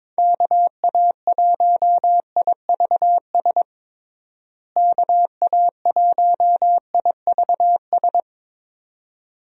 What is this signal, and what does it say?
Morse code, heard as a keyed tone, about 22 wpm, 700 Hz. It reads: KA1IVS KA1IVS
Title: USA Call Signs - 6 Characters - Repeated 2x - 22wpm